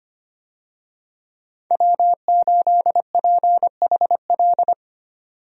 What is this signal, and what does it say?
Morse code, heard as a keyed tone, about 25 wpm, 700 Hz. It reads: W8PHL
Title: USA Call Signs - 5 Characters - 25wpm